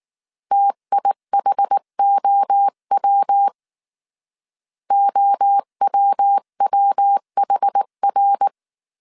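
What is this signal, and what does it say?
Morse code, heard as a keyed tone, about 19 wpm, 800 Hz. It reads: TIHOW OWWHR
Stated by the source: Morse code dictation at 18 wpm from the IW3SGT QRS Morse code training course, five-character groups of 300 characters.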